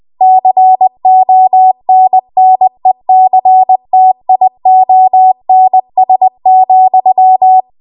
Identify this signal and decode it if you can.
Morse code, heard as a keyed tone, about 20 wpm, 750 Hz. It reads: CONNECTIONS,
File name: Morse / 20WPM